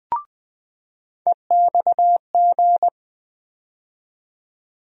Morse code, heard as a keyed tone, about 20 wpm, 700 Hz. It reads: EXG
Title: USA Call Sign Postfixes - 3 Letters 20wpm